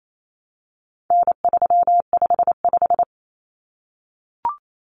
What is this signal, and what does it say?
Morse code, heard as a keyed tone, about 28 wpm, 700 Hz. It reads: N355
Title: Common Cut Numbers - 4-Digit 28wpm